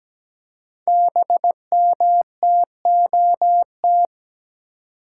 Morse code, heard as a keyed tone, about 17 wpm, 700 Hz. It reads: BMTOT
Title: Code Groups - 5 Letters - 17wpm